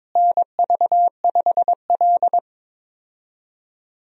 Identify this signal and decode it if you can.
Morse code, heard as a keyed tone, about 22 wpm, 700 Hz. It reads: NV5L
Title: USA Call Signs - 4 Characters - Repeated 3x - 22wpm